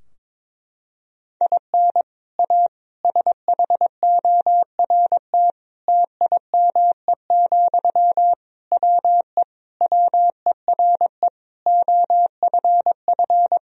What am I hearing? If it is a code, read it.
Morse code, heard as a keyed tone, about 21 wpm, 700 Hz. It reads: IN A SHORT TIME, WE WERE OFF
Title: Sentences from Top 1000 Words - Review Part 2 - 22wpm